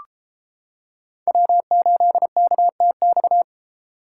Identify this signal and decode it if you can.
Morse code, heard as a keyed tone, about 30 wpm, 700 Hz. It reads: W8KTX